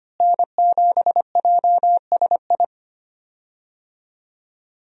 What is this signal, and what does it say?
Morse code, heard as a keyed tone, about 25 wpm, 690 Hz. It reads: N7JSI
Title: USA Call Signs - 5 Characters - Repeated 2x - 25wpm